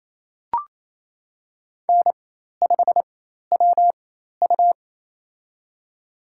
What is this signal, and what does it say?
Morse code, heard as a keyed tone, about 28 wpm, spaced as for 13 wpm, 700 Hz. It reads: N5WU
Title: USA Call Signs - 4 Characters - 4x Character Spacing - 28wpm